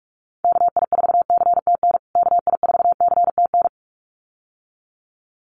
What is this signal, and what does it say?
Morse code, heard as a keyed tone, about 44 wpm, 700 Hz. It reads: KI4CTN KI4CTN
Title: USA Call Signs Repeated 2x 45wpm